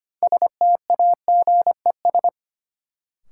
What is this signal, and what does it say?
Morse code, heard as a keyed tone, about 25 wpm, 700 Hz. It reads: STAGES
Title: Sets of 5 Words Encore - New Word Drill - 25wpm